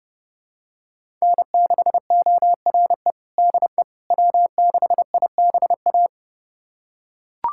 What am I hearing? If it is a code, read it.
Morse code, heard as a keyed tone, about 30 wpm, 700 Hz. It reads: N6ORE DE W6IBA
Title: USA Callsign DE Callsign - 5 Characters - 30wpm